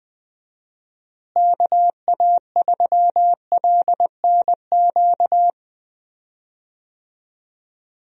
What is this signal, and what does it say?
Morse code, heard as a keyed tone, about 20 wpm, 700 Hz. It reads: KA3LNQ
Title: USA Call Signs - 6 Characters - Repeated 2x - 20wpm